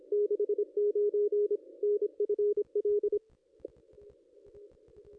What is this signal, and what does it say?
Morse code, heard as a keyed tone, about 25 wpm, 410 Hz. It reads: 69 NFL
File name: contests / ss-cw-k5tr-op-n5rz -2200